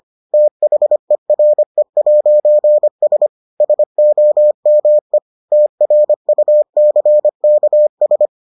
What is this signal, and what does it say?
Morse code, heard as a keyed tone, about 25 wpm, 600 Hz. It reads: THERE'S SOME TRUCKS